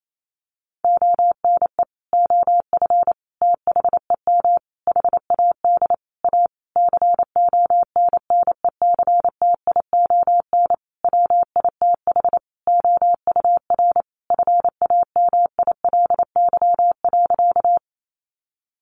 Morse code, heard as a keyed tone, about 28 wpm, 700 Hz. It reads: ONE OF THEM HAD A CONNECTION WITH OUR FAMILY.
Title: Sentences from Top 1300 Words 28wpm